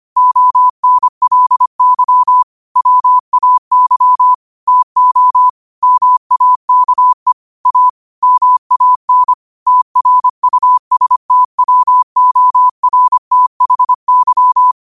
Morse code, heard as a keyed tone, about 25 wpm, 1000 Hz. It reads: ONLY WAY TO MAKE A MAN TRUSTWORTHY